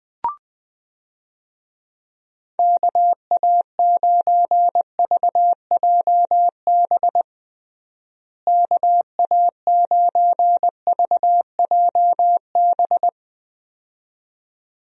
Morse code, heard as a keyed tone, about 21 wpm, 700 Hz. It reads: KA9VJB KA9VJB
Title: USA Call Signs - 6 Characters - Repeated 2x - 20wpm